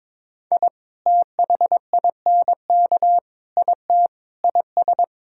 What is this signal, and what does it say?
Morse code, heard as a keyed tone, about 22 wpm, 700 Hz. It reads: I THINK IT IS
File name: practice / 22wpm / Sets of 8 Words - 22wpm